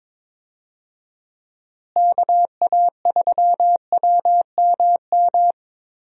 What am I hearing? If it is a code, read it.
Morse code, heard as a keyed tone, about 22 wpm, 700 Hz. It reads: KA3WMM
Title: USA Call Signs - 6 Characters - Repeated 2x - 22wpm